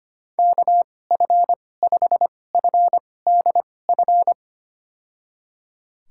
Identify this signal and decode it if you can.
Morse code, heard as a keyed tone, about 25 wpm, spaced as for 18 wpm, 700 Hz. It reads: KF5FDF
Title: USA Call Signs - 2x Character Spacing - 25wpm